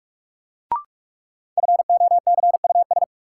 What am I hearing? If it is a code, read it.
Morse code, heard as a keyed tone, about 43 wpm, 700 Hz. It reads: FOCUS
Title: Sentences from Top 1000 Words - Review Part 4 - New Word Drill - 45wpm